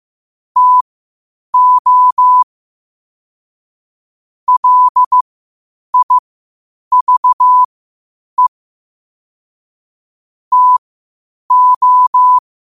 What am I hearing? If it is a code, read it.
Morse code, heard as a keyed tone, about 15 wpm, spaced as for 8 wpm, 1000 Hz. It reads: TO LIVE TO
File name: qotdfiles / 8wpm